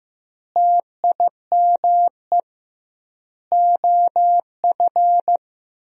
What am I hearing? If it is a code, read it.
Morse code, heard as a keyed tone, about 15 wpm, 700 Hz. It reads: TIME OF
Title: 2x Word-Spacing - Sentences from Top 300 Words - 15wpm